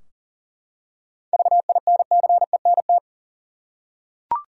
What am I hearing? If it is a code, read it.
Morse code, heard as a keyed tone, about 40 wpm, 700 Hz. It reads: VINCENT
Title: USA Common Names - 40wpm